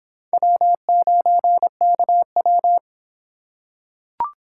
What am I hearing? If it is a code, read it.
Morse code, heard as a keyed tone, about 26 wpm, 700 Hz. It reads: W9KW